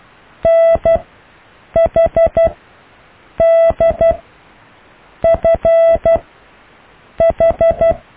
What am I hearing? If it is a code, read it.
Morse code, heard as a keyed tone, about 12 wpm, spaced as for 6 wpm, 660 Hz. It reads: NHDFH
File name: cw / EKTPV+BFHNW